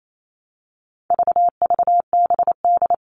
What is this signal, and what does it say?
Morse code, heard as a keyed tone, about 28 wpm, 700 Hz. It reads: VVBD
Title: All Cut Numbers - 4-Digit 28wpm